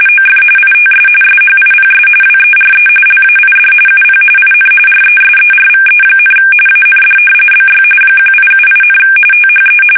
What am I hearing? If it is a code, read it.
RTTY, about 45.45 baud. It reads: OF ITS SHIPS WERE LAID UP IN RESERVE, AND BY 1878, THE
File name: fsk850 / usnavy1